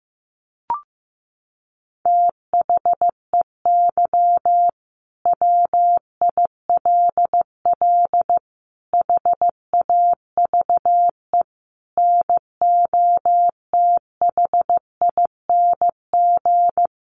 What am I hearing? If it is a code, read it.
Morse code, heard as a keyed tone, about 15 wpm, 700 Hz. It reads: THEY WILL HAVE NOTHING
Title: Sentences from Top 800 Words 15wpm